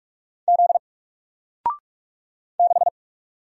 Morse code, heard as a keyed tone, about 44 wpm, 700 Hz. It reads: Z 6
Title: Single Letter-Number Alt 45wpm